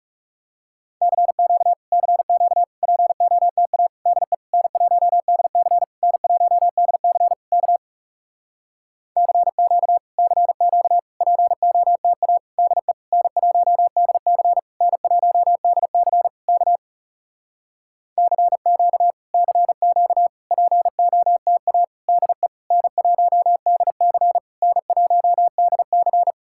Morse code, heard as a keyed tone, about 40 wpm, 700 Hz. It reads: CQ CQ POTA DE N1DC N1DC K CQ CQ POTA DE N1DC N1DC K CQ CQ POTA DE N1DC N1DC